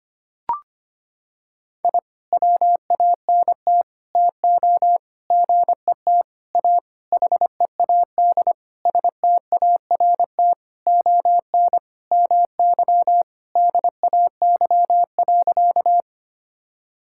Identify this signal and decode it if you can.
Morse code, heard as a keyed tone, about 25 wpm, 700 Hz. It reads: I WANT TO GET A HEAD START ON MY DAY.
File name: practice / 25wpm / Sentences with 9-Word Phrases - 25wpm